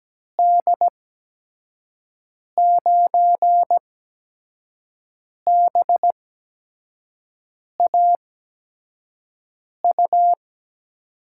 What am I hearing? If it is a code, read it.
Morse code, heard as a keyed tone, about 17 wpm, spaced as for 5 wpm, 700 Hz. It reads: D9BAU
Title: USA Call Signs - 8x Character Spacing - 17wpm